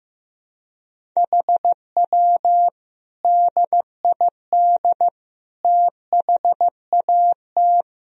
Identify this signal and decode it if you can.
Morse code, heard as a keyed tone, about 15 wpm, 700 Hz. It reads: HW DID THAT
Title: Sentences from Top 200 Words and Abbreviations 15wpm